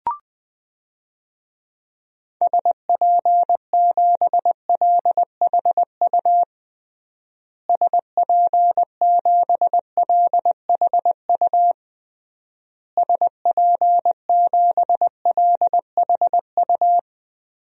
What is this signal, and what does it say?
Morse code, heard as a keyed tone, about 21 wpm, 700 Hz. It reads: SP7LHU SP7LHU SP7LHU
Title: DXCC-100 Call Signs - 6 Characters - Repeated 3x - 20wpm